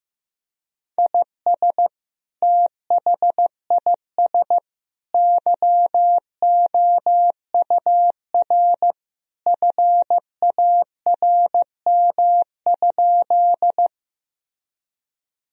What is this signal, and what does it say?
Morse code, heard as a keyed tone, about 15 wpm, 700 Hz. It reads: IS THIS YOUR FARM?
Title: Sentences from Top 1000 Words - Review Part 2 - 15wpm